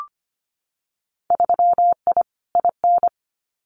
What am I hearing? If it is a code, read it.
Morse code, heard as a keyed tone, about 25 wpm, 700 Hz. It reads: 3I IN
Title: Winter Field Day Exchanges 25wpm